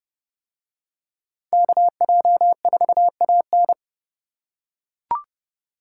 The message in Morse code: KJ4AN

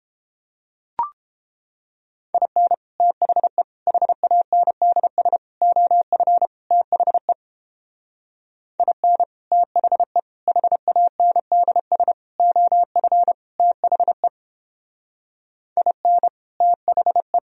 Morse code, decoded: IN THE HANDS OF THE IN THE HANDS OF THE IN THE

